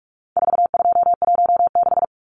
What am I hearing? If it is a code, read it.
Morse code, heard as a keyed tone, about 44 wpm, 700 Hz. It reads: 4216
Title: Numbers - 4-Digit 45wpm